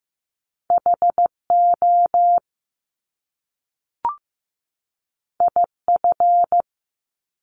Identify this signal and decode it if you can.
Morse code, heard as a keyed tone, about 15 wpm, 700 Hz. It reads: HO IF
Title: Top Two Letter Combinations in Words 15wpm